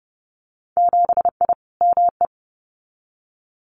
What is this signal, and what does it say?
Morse code, heard as a keyed tone, about 30 wpm, 700 Hz. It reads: 7I ME
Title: Winter Field Day Exchanges Encore 30wpm